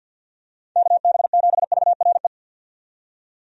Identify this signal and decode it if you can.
Morse code, heard as a keyed tone, about 52 wpm, 700 Hz. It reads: KB7VRE
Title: USA Call Signs - 6 Characters - 50wpm